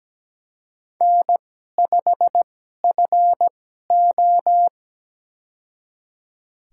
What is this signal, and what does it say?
Morse code, heard as a keyed tone, about 17 wpm, spaced as for 12 wpm, 700 Hz. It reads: N5FO
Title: USA Call Signs - 4 Characters - 2x Character Spacing - 17wpm